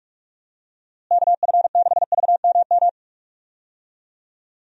Morse code, heard as a keyed tone, about 42 wpm, 700 Hz. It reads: KF6VMM
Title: USA Call Signs - 6 Characters - Repeated 3x - 45wpm